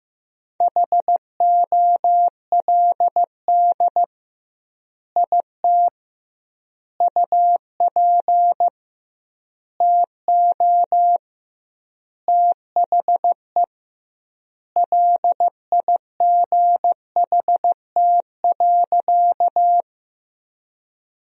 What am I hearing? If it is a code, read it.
Morse code, heard as a keyed tone, about 15 wpm, 700 Hz. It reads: HOLD IT UP TO THE LIGHT.
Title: 2x Word-Spacing - Sentences from Top 400 Words - 15wpm